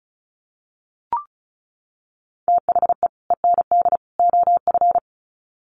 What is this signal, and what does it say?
Morse code, heard as a keyed tone, about 34 wpm, 700 Hz. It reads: THE END OF
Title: Sets of 3 Words 35wpm